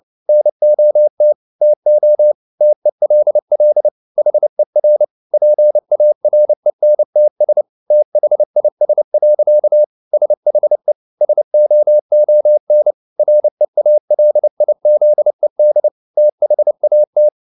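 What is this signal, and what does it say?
Morse code, heard as a keyed tone, about 29 wpm, 600 Hz. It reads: NOT TO TELL HER PARENTS THIS. SHE SOON REALIZED THAT
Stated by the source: Morse code at 29 wpm, sending Ken Follett's novel Never in English, from CW med Gnister.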